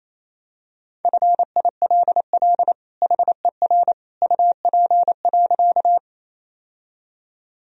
Morse code, heard as a keyed tone, about 28 wpm, 700 Hz. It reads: FILL HER UP.
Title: Sentences from Top 1000 Words - Review Part 3 - 28wpm